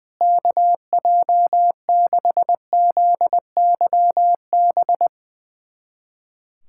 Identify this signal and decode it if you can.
Morse code, heard as a keyed tone, about 21 wpm, 700 Hz. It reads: KJ6ZYB
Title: USA Callsign DE Callsign - 6 Characters - 20wpm